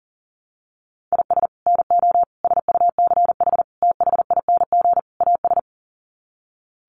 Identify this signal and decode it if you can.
Morse code, heard as a keyed tone, about 37 wpm, 700 Hz. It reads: IS NO SUCH THING AS
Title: Sets of 5 Words Encore 40wpm